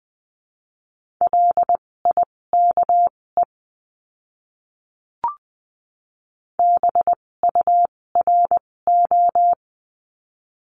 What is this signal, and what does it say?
Morse code, heard as a keyed tone, about 20 wpm, spaced as for 16 wpm, 700 Hz. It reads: LIKE BURO
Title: 256 Lesson 42a - Words 12-20wpm